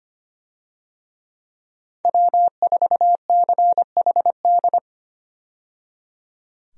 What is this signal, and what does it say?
Morse code, heard as a keyed tone, about 25 wpm, 700 Hz. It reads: W4CHD
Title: USA Call Signs - 5 Characters - 25wpm